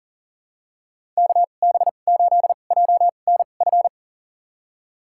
Morse code, heard as a keyed tone, about 40 wpm, spaced as for 29 wpm, 700 Hz. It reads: KB8JNF
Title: USA Call Signs - 2x Character Spacing - 40wpm